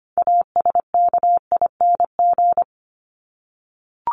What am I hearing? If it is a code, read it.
Morse code, heard as a keyed tone, about 25 wpm, 700 Hz. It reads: ASKING
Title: ING Suffix - Top 100 Words with Base Words 25wpm